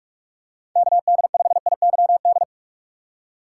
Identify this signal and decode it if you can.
Morse code, heard as a keyed tone, about 43 wpm, 700 Hz. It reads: KD5IYD
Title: USA Call Signs - 4x Character Spacing - 45wpm